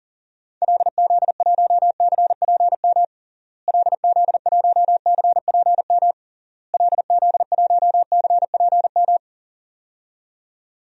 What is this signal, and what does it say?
Morse code, heard as a keyed tone, about 40 wpm, 700 Hz. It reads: LZ1CPM LZ1CPM LZ1CPM
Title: DXCC-100 Call Signs - 6 Characters - Repeated 3x - 40wpm